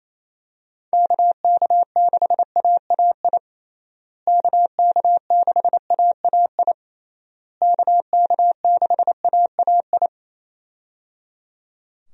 Morse code, heard as a keyed tone, about 28 wpm, 700 Hz. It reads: KK6AAI KK6AAI KK6AAI
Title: USA Call Signs - 6 Characters - Repeated 3x - 28wpm